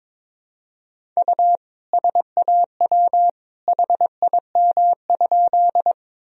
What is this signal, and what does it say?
Morse code, heard as a keyed tone, about 21 wpm, 700 Hz. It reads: U SAW HIM?
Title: Sentences from Top 300 Words and Abbreviations 22wpm